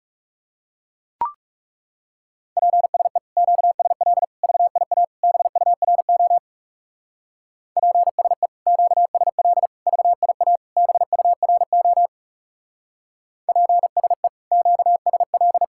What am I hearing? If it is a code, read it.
Morse code, heard as a keyed tone, about 37 wpm, 700 Hz. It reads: PSE QSL VIA BURO PSE QSL VIA BURO PSE QSL
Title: Speed-Racing Common QSO Phrases - 30wpm